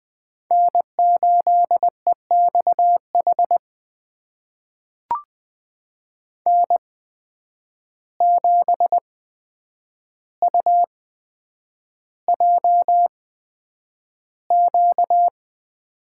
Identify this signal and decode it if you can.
Morse code, heard as a keyed tone, about 20 wpm, spaced as for 9 wpm, 700 Hz. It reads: N8EXH N7UJQ
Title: USA Call Signs - 5 Characters - 8x Character Spacing - 20wpm